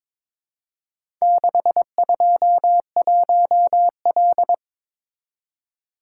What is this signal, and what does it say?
Morse code, heard as a keyed tone, about 22 wpm, 700 Hz. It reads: 621L
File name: practice / 22wpm / Four Letter-Number Combinations - 22wpm